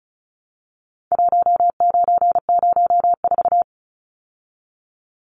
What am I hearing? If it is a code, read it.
Morse code, heard as a keyed tone, about 34 wpm, 700 Hz. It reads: 1904
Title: Numbers - 4-Digit 35wpm